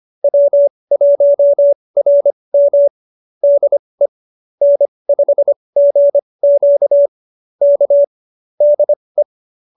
Morse code, heard as a keyed tone, about 25 wpm, spaced as for 20 wpm, 570 Hz. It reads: W1RM DE N5GQ K DE